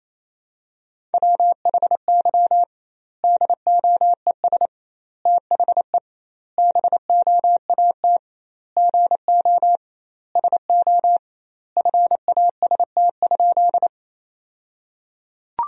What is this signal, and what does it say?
Morse code, heard as a keyed tone, about 28 wpm, 700 Hz. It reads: WHY DOES THE BOAT GO SO FAST?
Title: 2x Word-Spacing - Sentences from Top 500 Words - 28wpm